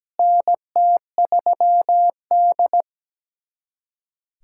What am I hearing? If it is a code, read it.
Morse code, heard as a keyed tone, about 17 wpm, 700 Hz. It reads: NT3D
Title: USA Callsign DE Callsign - 4 Characters - 17wpm